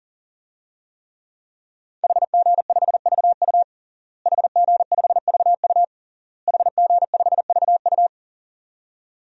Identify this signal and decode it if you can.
Morse code, heard as a keyed tone, about 40 wpm, 700 Hz. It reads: HG5VU HG5VU HG5VU
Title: DXCC-100 Call Signs - 5 Characters - Repeated 3x - 40wpm